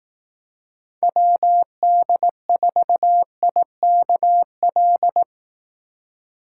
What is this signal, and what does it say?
Morse code, heard as a keyed tone, about 19 wpm, 700 Hz. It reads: WD4IKL
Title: Speed Racing - USA 2x3 Call Signs 15wpm